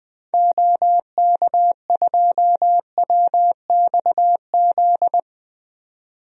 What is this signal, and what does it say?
Morse code, heard as a keyed tone, about 20 wpm, 700 Hz. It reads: OK2WXZ
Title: DXCC-100 Call Signs - 6 Characters - 20wpm